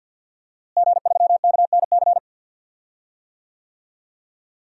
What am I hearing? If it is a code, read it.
Morse code, heard as a keyed tone, about 51 wpm, 700 Hz. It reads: G3KNC
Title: DXCC-100 Call Signs - 5 Characters - Repeated 3x - 50wpm